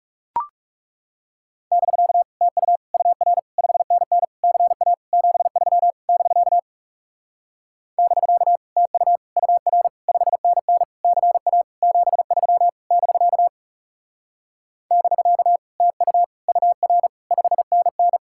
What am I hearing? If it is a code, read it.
Morse code, heard as a keyed tone, about 40 wpm, 700 Hz. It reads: <BK> TU UR 5NN CA 73 <BK> <BK> TU UR 5NN CA 73 <BK> <BK> TU UR 5NN